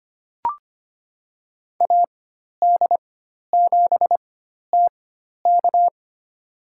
Morse code, heard as a keyed tone, about 25 wpm, spaced as for 12 wpm, 700 Hz. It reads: AD7TK
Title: USA Call Signs - 4x Character Spacing - 25wpm